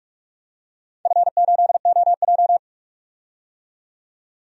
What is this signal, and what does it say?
Morse code, heard as a keyed tone, about 44 wpm, 700 Hz. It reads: F8OJ
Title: DXCC-100 Call Signs - 4 Characters - 45wpm